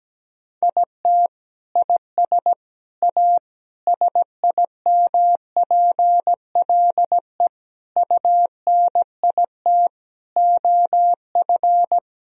Morse code, heard as a keyed tone, about 17 wpm, 700 Hz. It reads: IT IS A SIMPLE UNIT OF
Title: Sentences from Top 1000 Words - Review Part 2 - 17wpm